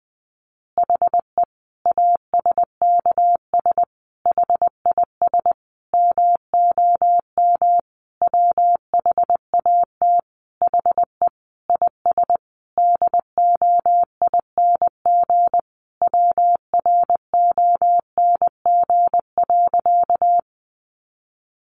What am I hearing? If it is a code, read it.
Morse code, heard as a keyed tone, about 20 wpm, 700 Hz. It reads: HE ASKS HIS MOM WHAT HE IS DOING WRONG.
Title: Easy Phrases 20wpm